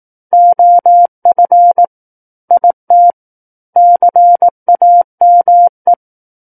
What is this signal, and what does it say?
Morse code, heard as a keyed tone, about 18 wpm, 700 Hz. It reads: OF IT CAME
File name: Book One / 09-Chapter 9 - 18 WPM